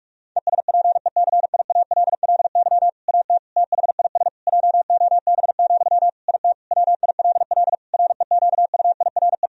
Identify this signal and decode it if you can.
Morse code, heard as a keyed tone, about 46 wpm, 700 Hz. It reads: ESPECIALLY AT THIS JOB, IT WILL REQUIRE
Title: Sentences from Top 1000 Words - Review Part 4 - 45wpm